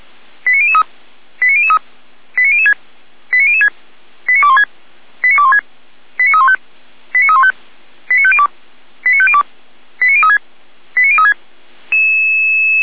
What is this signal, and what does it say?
Five-tone selective call (ZVEI), heard as quick runs of five tones, tones of about 70 ms. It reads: 89003 89003 89006 89006 89216 89216 89315 89315 89502 89502 89036 89036
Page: SDR/ZVEI